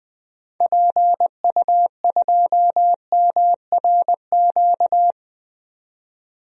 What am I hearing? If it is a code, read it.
Morse code, heard as a keyed tone, about 20 wpm, 700 Hz. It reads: PU2MRQ
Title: DXCC-100 Call Signs - 6 Characters - 20wpm